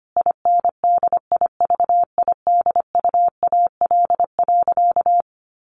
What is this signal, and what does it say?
Morse code, heard as a keyed tone, about 25 wpm, 700 Hz. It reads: INDIVIDUAL.
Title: Sentences from Top 1400 Words 25wpm